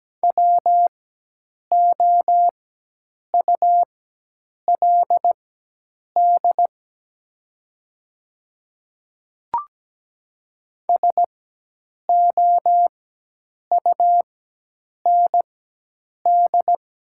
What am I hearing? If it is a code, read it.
Morse code, heard as a keyed tone, about 17 wpm, spaced as for 8 wpm, 700 Hz. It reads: WOULD SOUND
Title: 4x Character Spacing - Top 100 Words - 17wpm